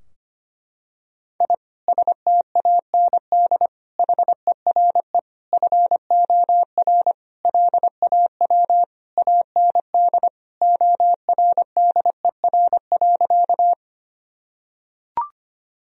Morse code, heard as a keyed tone, about 25 wpm, 700 Hz. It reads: I STAND HERE FOR LAW AND ORDER.
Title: Sentences from Top 1000 Words - Review Part 3 - 25wpm